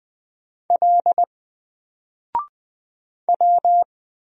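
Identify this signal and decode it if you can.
Morse code, heard as a keyed tone, about 20 wpm, 700 Hz. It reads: L W
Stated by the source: Single letter and number Morse drill at 20 wpm, from N4LCD.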